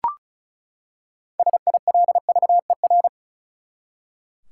Morse code, heard as a keyed tone, about 36 wpm, 700 Hz. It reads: SILVER
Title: Six Letter Words - 35wpm